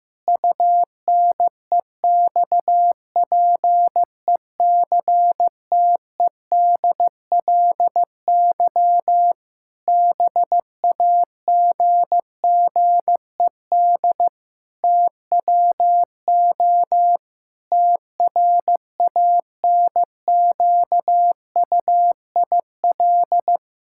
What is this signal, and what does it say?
Morse code, heard as a keyed tone, about 15 wpm, 700 Hz. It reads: UNEXPECTEDLY BAGGED TWO TRANQUIL
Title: Pangram Challenge v2 - 15wpm